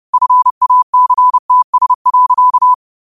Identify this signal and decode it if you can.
Morse code, heard as a keyed tone, about 30 wpm, 1000 Hz. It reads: FACTS.